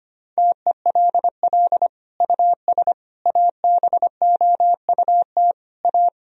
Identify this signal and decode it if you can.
Morse code, heard as a keyed tone, about 25 wpm, 700 Hz. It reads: TELL US ABOUT A
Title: Sentences from Top 1000 Words - Review Part 3 - 25wpm